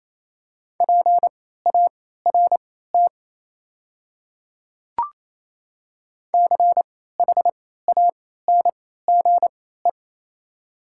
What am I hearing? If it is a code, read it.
Morse code, heard as a keyed tone, about 28 wpm, spaced as for 16 wpm, 700 Hz. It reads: PART CHANGE
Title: 3x Character Spacing - Top 200 Words - 28wpm